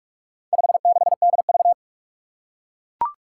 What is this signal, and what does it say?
Morse code, heard as a keyed tone, about 41 wpm, 700 Hz. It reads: 56DV
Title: Four Challenging Letter-Number Combinations 45wpm